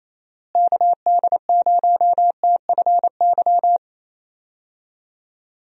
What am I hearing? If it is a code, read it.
Morse code, heard as a keyed tone, about 28 wpm, 700 Hz. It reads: KD0TFY
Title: USA Call Signs - 6 Characters - Repeated 2x - 28wpm